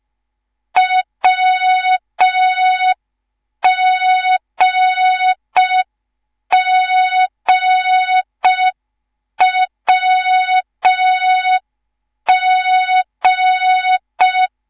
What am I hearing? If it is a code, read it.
Morse code, heard as a keyed tone, about 5 wpm, 3700 Hz. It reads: WGGWG
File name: Morze-HA2MN / hang / 10GWZ HA2MN